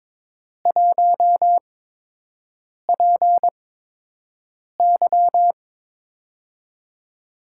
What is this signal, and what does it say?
Morse code, heard as a keyed tone, about 22 wpm, spaced as for 6 wpm, 700 Hz. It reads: 1PY
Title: USA Call Signs - 4 Characters - 8x Character Spacing - 22wpm